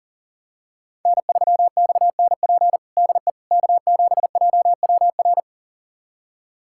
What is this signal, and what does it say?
Morse code, heard as a keyed tone, about 40 wpm, 700 Hz. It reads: N3XNP DE K7JWR
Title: USA Callsign DE Callsign - 5 Characters - 40wpm